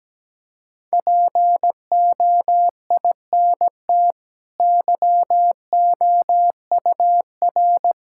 Morse code, heard as a keyed tone, about 17 wpm, 700 Hz. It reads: POINT YOUR
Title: Sentences from Top 1600 Words 17wpm